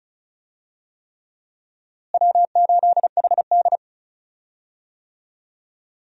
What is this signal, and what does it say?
Morse code, heard as a keyed tone, about 34 wpm, 700 Hz. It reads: W8HD
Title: USA Call Signs - 4 Characters - Repeated 3x - 35wpm